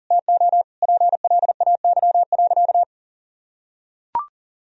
Morse code, heard as a keyed tone, about 40 wpm, 700 Hz. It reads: TO PLAY.
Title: Sentences from Top 1000 Words - Review Part 1 - 40wpm